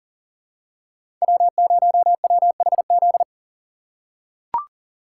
Morse code, heard as a keyed tone, about 40 wpm, 700 Hz. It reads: W0WHZ
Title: USA Call Signs - 3x Character Spacing - 40wpm